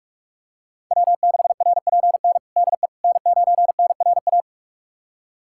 Morse code, heard as a keyed tone, about 47 wpm, 700 Hz. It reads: W6RPN DE N9NRA
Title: USA Callsign DE Callsign - 5 Characters - 45wpm